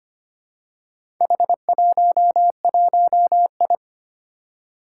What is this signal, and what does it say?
Morse code, heard as a keyed tone, about 25 wpm, 700 Hz. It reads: H11I
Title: Four Challenging Letter-Number Combinations 25wpm